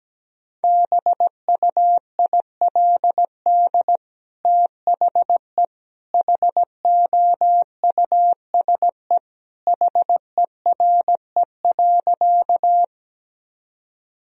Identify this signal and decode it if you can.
Morse code, heard as a keyed tone, about 17 wpm, 700 Hz. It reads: BUILD THE HOUSE HERE.
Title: Sentences from Top 1000 Words - Review Part 1 - 17wpm